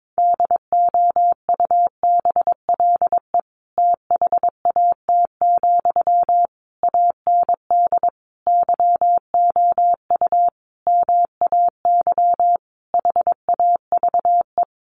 Morse code, heard as a keyed tone, about 22 wpm, 700 Hz. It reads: DOUBLE THAT, AND YOU MAY HAVE
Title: Sentences from Top 1400 Words 22wpm